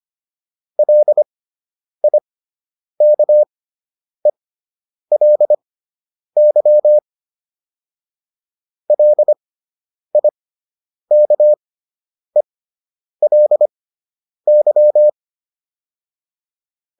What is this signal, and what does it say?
Morse code, heard as a keyed tone, about 25 wpm, spaced as for 9 wpm, 600 Hz. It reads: LIKELY LIKELY